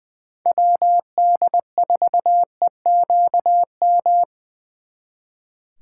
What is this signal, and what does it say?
Morse code, heard as a keyed tone, about 24 wpm, 700 Hz. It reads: WD4EQM
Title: Speed Racing - USA 2x3 Call Signs 20wpm